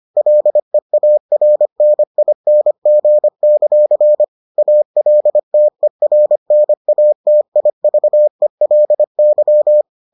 Morse code, heard as a keyed tone, about 25 wpm, 600 Hz. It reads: LEARNING; ALTERNATIVELY